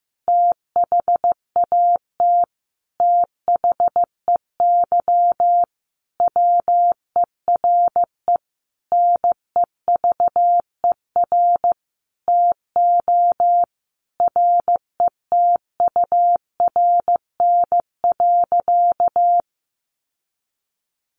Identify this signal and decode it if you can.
Morse code, heard as a keyed tone, about 15 wpm, 700 Hz. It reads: THAT THEY WERE NEVER TO RETURN.
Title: Sentences from Top 800 Words 15wpm